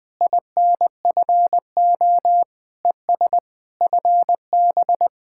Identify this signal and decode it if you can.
Morse code, heard as a keyed tone, about 20 wpm, 700 Hz. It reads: INFO ES FB
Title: Speed-Racing Common QSO Phrases - 20wpm